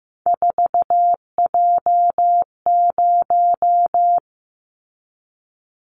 Morse code, heard as a keyed tone, about 15 wpm, 700 Hz. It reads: 4J0
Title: Callsign Prefixes for Top 100 DXCC - 15wpm